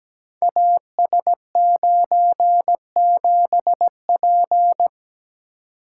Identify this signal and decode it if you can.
Morse code, heard as a keyed tone, about 17 wpm, 700 Hz. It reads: AS97P
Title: Code Groups - 5 Letter:Numbers - 17wpm